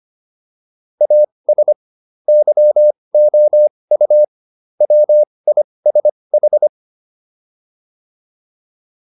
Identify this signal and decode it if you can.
Morse code, heard as a keyed tone, about 25 wpm, spaced as for 20 wpm, 600 Hz. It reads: AS YOU WISH